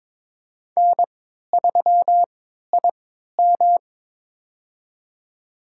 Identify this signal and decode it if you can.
Morse code, heard as a keyed tone, about 23 wpm, spaced as for 13 wpm, 700 Hz. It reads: N3IM
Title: USA Call Signs - 3x Character Spacing - 22wpm